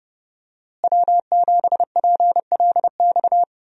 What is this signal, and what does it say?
Morse code, heard as a keyed tone, about 32 wpm, 700 Hz. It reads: W7PLX
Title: USA Call Signs - 5 Characters - Repeated 3x - 30wpm